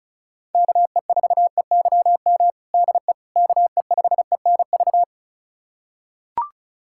Morse code, heard as a keyed tone, about 34 wpm, 700 Hz. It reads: KE4EYM DE KE5ENV
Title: USA Callsign DE Callsign - 6 Characters - 35wpm